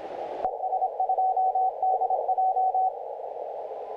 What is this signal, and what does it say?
Morse code, heard as a keyed tone, about 26 wpm, 750 Hz. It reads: ES2DJ
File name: audio mp3 files / YK9G 40m CW 2008-4-12 1-21-51